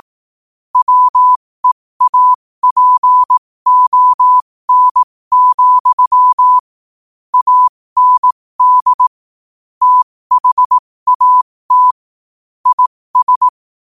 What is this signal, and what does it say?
Morse code, heard as a keyed tone, about 18 wpm, spaced as for 15 wpm, 1000 Hz. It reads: WEAPON, AND THAT IS
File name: qotdfiles / 15wpm